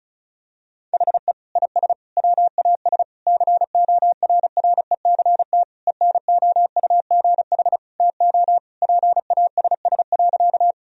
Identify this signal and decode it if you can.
Morse code, heard as a keyed tone, about 36 wpm, 700 Hz. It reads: HE IS WAS CORRECT ENOUGH TO PASS.
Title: Sentences from Top 1000 Words - Review Part 2 - 35wpm